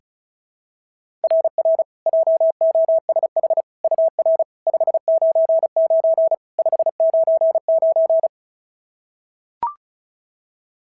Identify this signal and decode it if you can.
Morse code, heard as a keyed tone, about 33 wpm, 650 Hz. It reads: RR JOSH UR 599 599
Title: SOTA QSO Practice 35wpm